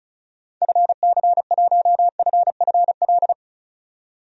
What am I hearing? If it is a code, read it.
Morse code, heard as a keyed tone, about 36 wpm, 700 Hz. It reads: FC1FFL
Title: DXCC-100 Call Signs - 6 Characters - 35wpm